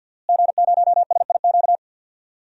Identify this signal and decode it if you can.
Morse code, heard as a keyed tone, about 51 wpm, 700 Hz. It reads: G0SIQ
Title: DXCC-100 Call Signs - 5 Characters - Repeated 2x - 50wpm